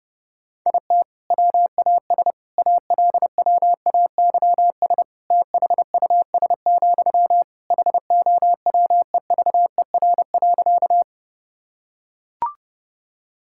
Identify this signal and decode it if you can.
Morse code, heard as a keyed tone, about 28 wpm, 700 Hz. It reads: IT WAS ALWAYS THUS, HOWEVER.
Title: Sentences from Top 1000 Words - Review Part 5 - 30wpm